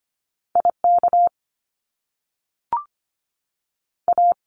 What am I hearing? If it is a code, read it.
Morse code, heard as a keyed tone, about 25 wpm, 700 Hz. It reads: IK A
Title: Top Two Letter Combinations in Words 25wpm